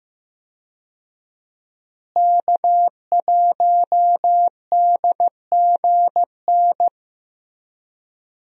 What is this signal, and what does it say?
Morse code, heard as a keyed tone, about 15 wpm, 700 Hz. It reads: K1DGN
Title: USA Call Signs - 5 Characters - Repeated 2x - 15wpm